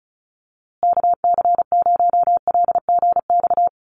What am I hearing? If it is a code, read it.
Morse code, heard as a keyed tone, about 34 wpm, 700 Hz. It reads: KC0LGX
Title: USA Call Signs Repeated 2x 35wpm